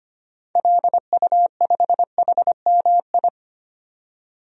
Morse code, heard as a keyed tone, about 25 wpm, 700 Hz. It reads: LU5HMI
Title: DXCC-100 Call Signs - 6 Characters - 25wpm